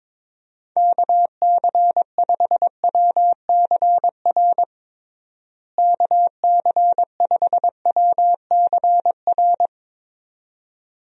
Morse code, heard as a keyed tone, about 22 wpm, 700 Hz. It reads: KC5WCR KC5WCR
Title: USA Call Signs - 6 Characters - Repeated 2x - 22wpm